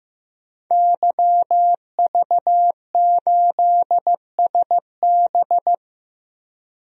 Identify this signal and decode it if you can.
Morse code, heard as a keyed tone, about 15 wpm, 700 Hz. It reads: YV8SB
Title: DXCC-100 Call Signs - 5 Characters - 15wpm